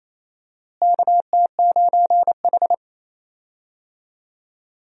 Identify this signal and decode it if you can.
Morse code, heard as a keyed tone, about 27 wpm, 700 Hz. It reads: KT9H
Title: USA Call Signs - 4 Characters - Repeated 3x - 28wpm